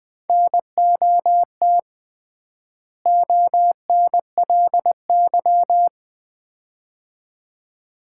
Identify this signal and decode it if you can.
Morse code, heard as a keyed tone, about 21 wpm, 700 Hz. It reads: NOT ONLY
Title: 3x Word-Spacing - Sets of 2 Words - 20wpm